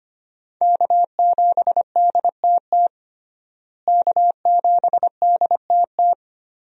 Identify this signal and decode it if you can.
Morse code, heard as a keyed tone, about 25 wpm, 700 Hz. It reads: K7DTT K7DTT
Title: USA Call Signs - 5 Characters - Repeated 3x - 25wpm